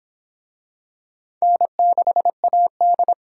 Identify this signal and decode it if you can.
Morse code, heard as a keyed tone, about 23 wpm, 700 Hz. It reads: N6AD